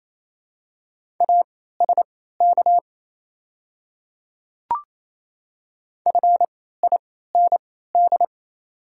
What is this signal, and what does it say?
Morse code, heard as a keyed tone, about 28 wpm, spaced as for 16 wpm, 700 Hz. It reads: ASK FIND
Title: 3x Character Spacing - Top 200 Words - 28wpm